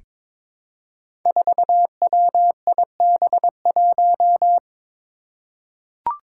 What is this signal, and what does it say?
Morse code, heard as a keyed tone, about 22 wpm, 700 Hz. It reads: 4WIB1
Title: Five Challenging Letter-Number Combinations 22wpm